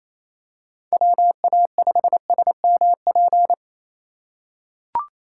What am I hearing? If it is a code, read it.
Morse code, heard as a keyed tone, about 28 wpm, 700 Hz. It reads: WA5SMP
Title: USA Call Signs - 6 Characters - 2x Character Spacing - 28wpm